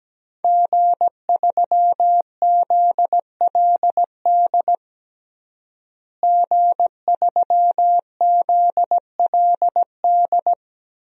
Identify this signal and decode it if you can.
Morse code, heard as a keyed tone, about 17 wpm, 700 Hz. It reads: G3ZLD G3ZLD
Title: DXCC-100 Call Signs - 5 Characters - Repeated 3x - 17wpm